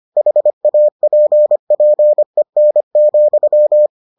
Morse code, heard as a keyed tone, about 25 wpm, 600 Hz. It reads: HAPPEN,